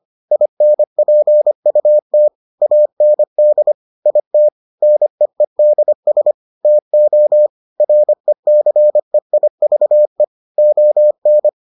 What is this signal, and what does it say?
Morse code, heard as a keyed tone, about 25 wpm, 600 Hz. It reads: INPUT AND IT NEEDS TO RECEIVE ON